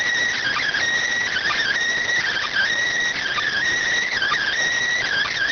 Emergency Alert System (SAME) tones, heard as harsh buzzing bursts. It is heard from 0.3 to 4.5 s.